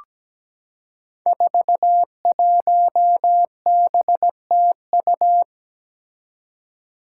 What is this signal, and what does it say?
Morse code, heard as a keyed tone, about 17 wpm, 700 Hz. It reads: 41BTU
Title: Five Challenging Letter-Number Combinations 17wpm